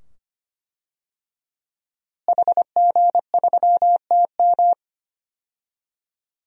Morse code, heard as a keyed tone, about 25 wpm, 700 Hz. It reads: HG3TM
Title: DXCC-100 Call Signs - 5 Characters - Repeated 2x - 25wpm